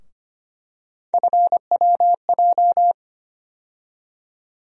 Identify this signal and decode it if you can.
Morse code, heard as a keyed tone, about 25 wpm, 700 Hz. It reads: FWJ
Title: USA Call Sign Postfixes - 3 Letters 25wpm